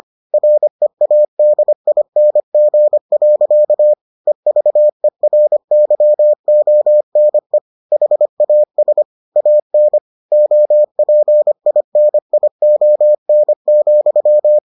Morse code, heard as a keyed tone, about 25 wpm, 600 Hz. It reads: READING. EVERYONE HAS AN OPINION,